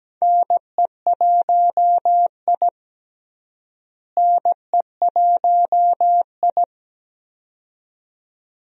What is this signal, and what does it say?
Morse code, heard as a keyed tone, about 17 wpm, 700 Hz. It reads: NE1I NE1I
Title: USA Call Signs - 4 Characters - Repeated 2x - 17wpm